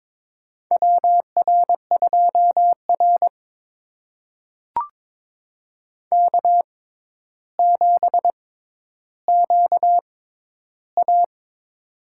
Morse code, heard as a keyed tone, about 22 wpm, spaced as for 11 wpm, 700 Hz. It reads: WR2R K7QA
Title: USA Call Signs - 4 Characters - 6x Character Spacing - 22wpm